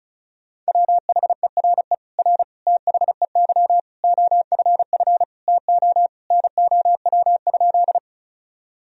Morse code, heard as a keyed tone, about 34 wpm, 700 Hz. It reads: WHERE R THEY OFF TO NOW?
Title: Sentences from Top 200 Words and Abbreviations 35wpm